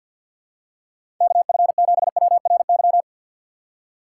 Morse code, heard as a keyed tone, about 52 wpm, 700 Hz. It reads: KF7WRY
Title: USA Call Signs - 6 Characters - 4x Character Spacing - 50wpm